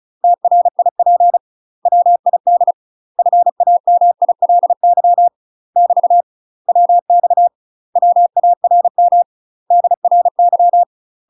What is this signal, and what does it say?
Morse code, heard as a keyed tone, about 36 wpm, 700 Hz. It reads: TRIP WID FAMILY <BT> WX WARM DRY
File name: UK MASTER QSOs TRAINING / 35WPM / UK MASTER QSOs 0011